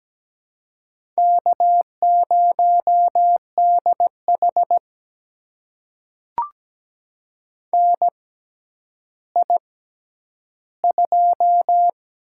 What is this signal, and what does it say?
Morse code, heard as a keyed tone, about 17 wpm, spaced as for 10 wpm, 700 Hz. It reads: K0DH NI2
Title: USA Call Signs - 4 Characters - 6x Character Spacing - 17wpm